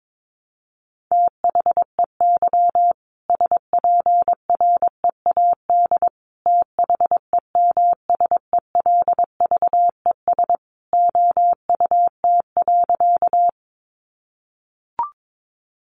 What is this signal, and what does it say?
Morse code, heard as a keyed tone, about 22 wpm, 700 Hz. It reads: THEY SPREAD THEMSELVES OUT.
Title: Sentences from Top 1200 Words 22wpm